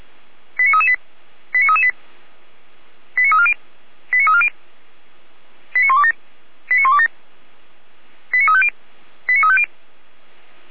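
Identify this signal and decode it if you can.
Five-tone selective call (ZVEI), heard as quick runs of five tones, tones of about 70 ms. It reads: 89338 89338 89340 89340 89127 89127 89350 89350
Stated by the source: ZVEI five-tone selective call 89305, received by SDR, from the audio archive of Feuerwehr Orlamünde (fire brigade).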